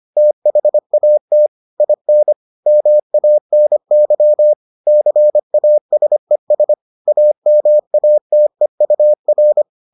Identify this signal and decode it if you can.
Morse code, heard as a keyed tone, about 25 wpm, 600 Hz. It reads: THAT IN MANY CASES AMATEUR